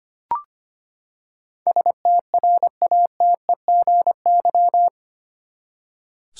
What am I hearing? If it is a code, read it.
Morse code, heard as a keyed tone, about 25 wpm, 700 Hz. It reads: STRATEGY
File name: Eight Letter Words - 25wpm